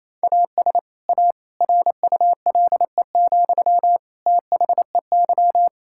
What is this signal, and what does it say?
Morse code, heard as a keyed tone, about 27 wpm, 700 Hz. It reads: AS A RULE, THEY
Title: Sentences from Top 1000 Words - Review Part 6 - 28wpm